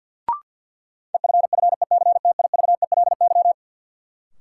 Morse code, heard as a keyed tone, about 53 wpm, 700 Hz. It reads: EFFECTIVELY
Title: Sentences from Top 1400 Words - New Word Drill - 50wpm